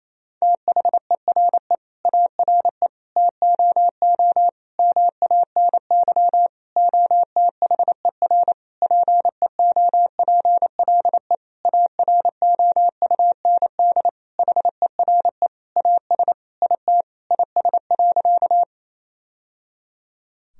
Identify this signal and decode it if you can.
Morse code, heard as a keyed tone, about 28 wpm, 700 Hz. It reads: THERE ARE TOO MANY OTHER PEOPLE AROUND HERE AS IT IS.
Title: Sentences from Top 1000 Words - Review Part 3 - 28wpm